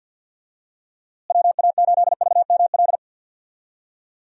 Morse code, heard as a keyed tone, about 50 wpm, 700 Hz. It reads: WA8VML
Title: USA Call Signs - 6 Characters - Repeated 3x - 50wpm